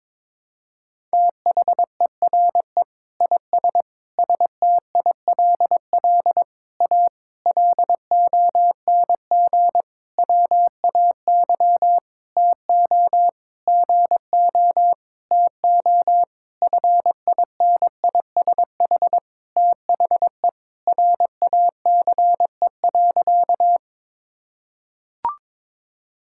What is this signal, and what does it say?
Morse code, heard as a keyed tone, about 21 wpm, 700 Hz. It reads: THERE IS STILL A LONG WAY TO GO TO FINISH THE RACE.
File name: practice / 22wpm / Sentences with 8-Word Phrases - 22wpm